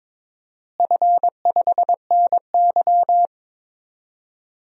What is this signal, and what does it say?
Morse code, heard as a keyed tone, about 22 wpm, 700 Hz. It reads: F5NY
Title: DXCC-100 Call Signs - 4 Characters - Repeated 2x - 22wpm